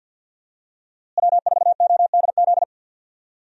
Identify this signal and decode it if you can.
Morse code, heard as a keyed tone, about 48 wpm, 700 Hz. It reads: W4ODZ
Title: USA Call Signs - 5 Characters - Repeated 3x - 50wpm